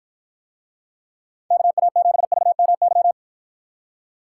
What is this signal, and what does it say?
Morse code, heard as a keyed tone, about 53 wpm, 700 Hz. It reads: KA7VMY